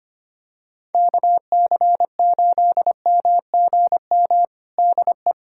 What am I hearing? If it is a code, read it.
Morse code, heard as a keyed tone, about 25 wpm, 700 Hz. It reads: KC8MGM DE
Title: USA Callsign DE Callsign - 6 Characters - 25wpm